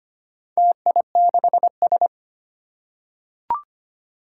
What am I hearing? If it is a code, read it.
Morse code, heard as a keyed tone, about 25 wpm, 700 Hz. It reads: TI6S
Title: Four Challenging Letter-Number Combinations 25wpm